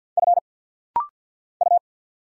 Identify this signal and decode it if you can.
Morse code, heard as a keyed tone, about 52 wpm, 700 Hz. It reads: F U
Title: Single Letters - Rapid-Fire 50wpm